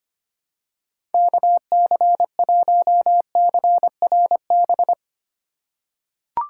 Morse code, heard as a keyed tone, about 25 wpm, 700 Hz. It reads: KC1CRB
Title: USA Call Signs - 6 Characters - 3x Character Spacing - 25wpm